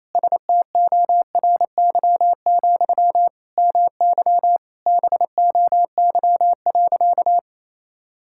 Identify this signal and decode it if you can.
Morse code, heard as a keyed tone, about 28 wpm, 690 Hz. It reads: STORY, MY BOY.
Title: Sentences from Top 1000 Words - Review Part 6 - 28wpm